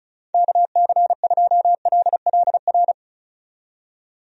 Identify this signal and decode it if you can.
Morse code, heard as a keyed tone, about 34 wpm, 700 Hz. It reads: KC2LLR
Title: USA Call Signs - 6 Characters - 35wpm